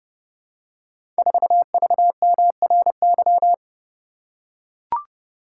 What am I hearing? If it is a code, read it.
Morse code, heard as a keyed tone, about 30 wpm, 700 Hz. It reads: 4VMRY
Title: Code Groups - 5 Letter:Numbers - 30wpm